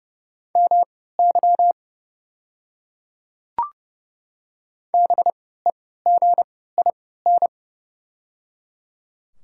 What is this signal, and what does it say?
Morse code, heard as a keyed tone, about 30 wpm, spaced as for 17 wpm, 690 Hz. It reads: MY BEGIN
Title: 3x Character Spacing - Top 300 Words - 30wpm